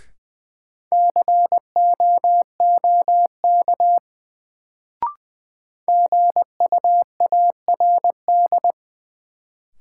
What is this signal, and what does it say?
Morse code, heard as a keyed tone, about 20 wpm, 700 Hz. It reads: COOK GUARD
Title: Sentences from Top 700 Words - New Word Drill - 20wpm